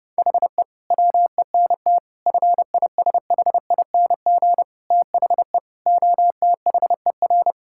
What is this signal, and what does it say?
Morse code, heard as a keyed tone, about 30 wpm, 700 Hz. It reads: HE WENT FISHING THE OTHER
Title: Sentences from Top 1000 Words - Review Part 5 - 30wpm